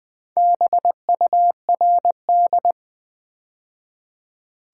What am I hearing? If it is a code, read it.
Morse code, heard as a keyed tone, about 20 wpm, 700 Hz. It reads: BURD
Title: Build A Word - 2 Letters At A Time - 6-Letter Words - 20wpm